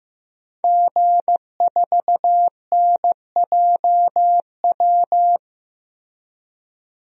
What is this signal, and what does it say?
Morse code, heard as a keyed tone, about 15 wpm, 700 Hz. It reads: G4NJW
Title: DXCC-100 Call Signs - 5 Characters - 15wpm